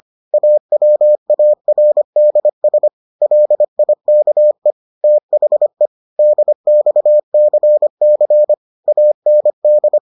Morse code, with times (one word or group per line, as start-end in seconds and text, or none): 0.3-2.9 AWARDS
3.2-4.7 LIKE
5.0-5.9 THE
6.2-8.5 DXCC
8.9-10.0 AND